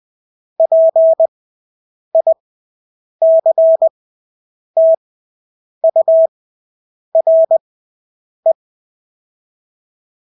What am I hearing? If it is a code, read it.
Morse code, heard as a keyed tone, about 20 wpm, spaced as for 8 wpm, 650 Hz. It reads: PICTURE
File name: Engelske ord 20/8 wpm